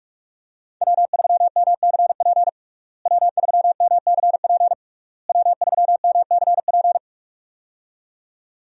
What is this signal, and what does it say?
Morse code, heard as a keyed tone, about 46 wpm, 700 Hz. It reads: W3MCP W3MCP W3MCP
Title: USA Call Signs - 5 Characters - Repeated 3x - 45wpm